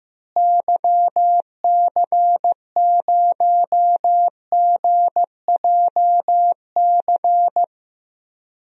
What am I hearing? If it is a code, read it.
Morse code, heard as a keyed tone, about 15 wpm, 700 Hz. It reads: YC0GJC
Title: DXCC-100 Call Signs - 6 Characters - Repeated 3x - 15wpm